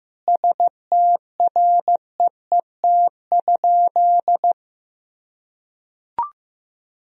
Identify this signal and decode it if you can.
Morse code, heard as a keyed tone, about 15 wpm, 700 Hz. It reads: STREET?
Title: Sentences from Top 1000 Words - Review Part 2 - 15wpm